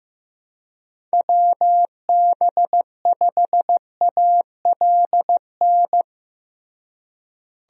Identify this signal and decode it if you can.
Morse code, heard as a keyed tone, about 15 wpm, 700 Hz. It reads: WB5ALN
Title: USA Call Signs - 6 Characters - Repeated 2x - 15wpm